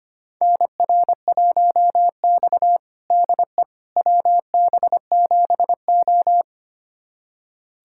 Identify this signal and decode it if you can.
Morse code, heard as a keyed tone, about 25 wpm, 700 Hz. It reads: NR1X DE WB7O
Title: USA Callsign DE Callsign - 4 Characters - 25wpm